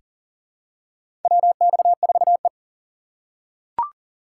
Morse code, heard as a keyed tone, about 40 wpm, 700 Hz. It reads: WX4E
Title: USA Call Signs - 4 Characters - 2x Character Spacing - 40wpm